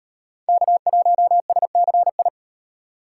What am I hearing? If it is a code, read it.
Morse code, heard as a keyed tone, about 38 wpm, 700 Hz. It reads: K1SCI